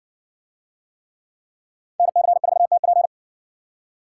Morse code, heard as a keyed tone, about 58 wpm, 700 Hz. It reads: NC4TP